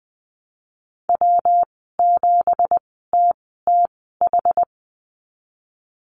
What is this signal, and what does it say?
Morse code, heard as a keyed tone, about 19 wpm, spaced as for 14 wpm, 700 Hz. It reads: W7TTH
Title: 173 Lesson 28g - Number 9 - Call Signs 20wpm